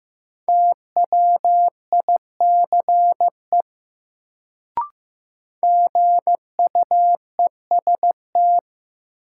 Morse code, heard as a keyed tone, about 15 wpm, 700 Hz. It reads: TWICE GUEST
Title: Sentences from Top 1600 Words - New Word Drill - 15wpm